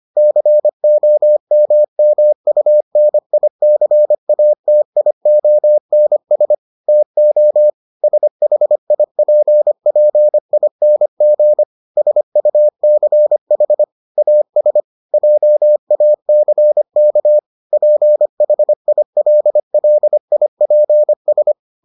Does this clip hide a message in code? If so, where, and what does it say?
Yes: Morse code from 0.2 to 21.5 s, heard as a keyed tone, about 25 wpm, 600 Hz. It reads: COMMUNICATIONS TO SHIPPING SUCH AS JACK PHILLIPS